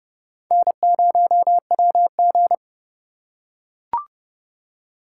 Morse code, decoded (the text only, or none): N0WG